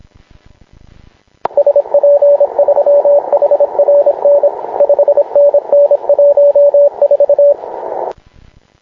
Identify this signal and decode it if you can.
Morse code, heard as a keyed tone, about 26 wpm, 570 Hz, with radio noise. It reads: SP3HRN 5NN 14
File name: klub / sat 0144 dj6qt ve1zj